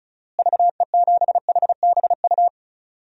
Morse code, heard as a keyed tone, about 34 wpm, 700 Hz. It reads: VE7HBU
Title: DXCC-100 Call Signs - 6 Characters - Repeated 2x - 35wpm